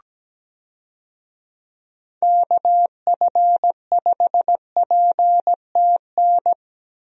Morse code, heard as a keyed tone, about 17 wpm, 700 Hz. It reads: KF5PTN
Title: USA Call Signs - 6 Characters - 17wpm